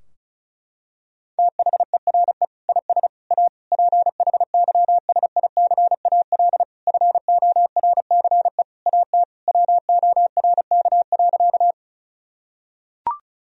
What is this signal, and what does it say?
Morse code, heard as a keyed tone, about 36 wpm, 700 Hz. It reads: THERE IS A PHYSICAL FORCE AT WORK.
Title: Sentences from Top 1000 Words - Review Part 4 - 35wpm